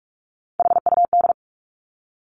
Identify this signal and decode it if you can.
Morse code, heard as a keyed tone, about 46 wpm, 700 Hz. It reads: HUD
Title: Three Challenging Letter-Number Combinations 45wpm